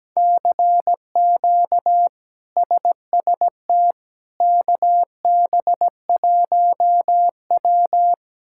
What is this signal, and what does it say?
Morse code, heard as a keyed tone, about 17 wpm, 700 Hz. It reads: CQ SST KB1W
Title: K1USN SST QSO Practice - 17wpm